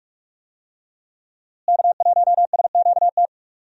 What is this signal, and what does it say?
Morse code, heard as a keyed tone, about 46 wpm, 700 Hz. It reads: K1SQT
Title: USA Call Signs - 5 Characters - Repeated 3x - 45wpm